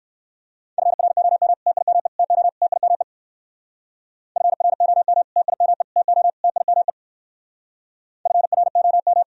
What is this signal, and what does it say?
Morse code, heard as a keyed tone, about 62 wpm, 700 Hz. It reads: FROM TIME TO TIME FROM TIME TO TIME FROM